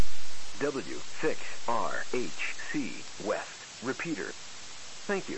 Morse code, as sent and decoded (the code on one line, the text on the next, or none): none